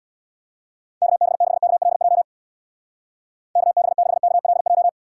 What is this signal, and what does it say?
Morse code, heard as a keyed tone, about 64 wpm, 700 Hz. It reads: KB6KLJ KB6KLJ